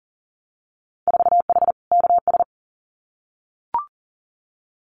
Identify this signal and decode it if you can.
Morse code, heard as a keyed tone, about 40 wpm, 700 Hz. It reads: 4H KS
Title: Winter Field Day Exchanges Encore 40wpm